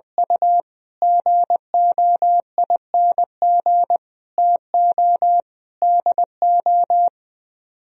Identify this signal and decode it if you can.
Morse code, heard as a keyed tone, about 20 wpm, 700 Hz. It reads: U GOING TO DO